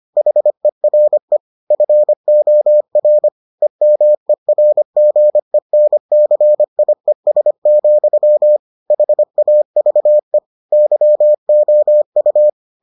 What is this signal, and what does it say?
Morse code, heard as a keyed tone, about 25 wpm, 600 Hz. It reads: HERE FOR EMERGENCIES, HAVE YOU